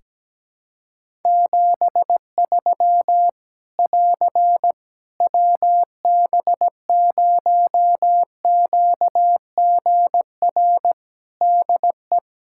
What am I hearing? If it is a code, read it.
Morse code, heard as a keyed tone, about 17 wpm, 700 Hz. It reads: 73 <AR> WB0QGR DE
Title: Practice QSO - Exchange 4 - Ending - 17wpm